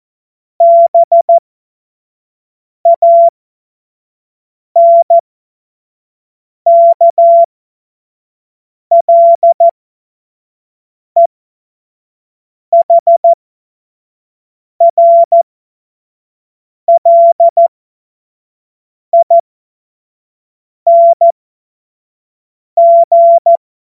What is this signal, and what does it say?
Morse code, heard as a keyed tone, about 14 wpm, spaced as for 5 wpm, 680 Hz. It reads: BANKLEHRLING
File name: Witz-08-Juli-2025-25-F70